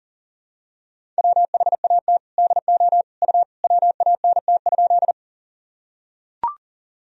Morse code, heard as a keyed tone, about 40 wpm, 700 Hz. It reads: WHAT DO U WANT?
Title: Sentences from Top 200 Words and Abbreviations 40wpm